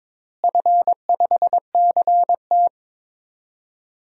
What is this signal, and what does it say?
Morse code, heard as a keyed tone, about 22 wpm, 700 Hz. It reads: F5CT
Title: DXCC-100 Call Signs - 4 Characters - Repeated 2x - 22wpm